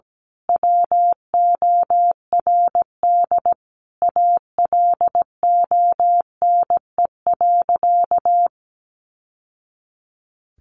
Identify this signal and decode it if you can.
Morse code, heard as a keyed tone, about 17 wpm, 700 Hz. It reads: WORD ALONE.
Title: Sentences from Top 1000 Words 17wpm